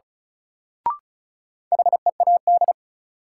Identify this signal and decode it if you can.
Morse code, heard as a keyed tone, about 36 wpm, 700 Hz. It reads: HEAD
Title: Sentences from Top 300 Words - New Word Drill - 35wpm